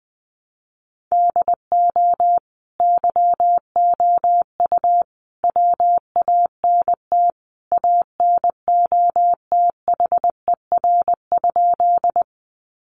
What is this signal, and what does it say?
Morse code, heard as a keyed tone, about 20 wpm, 700 Hz. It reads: DO YOU WANT ANOTHER?
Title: Sentences from Top 600 Words 20wpm